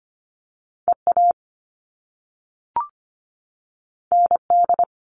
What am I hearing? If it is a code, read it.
Morse code, heard as a keyed tone, about 25 wpm, 700 Hz. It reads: EA ND
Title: All Cut Numbers - 2-Digit 25wpm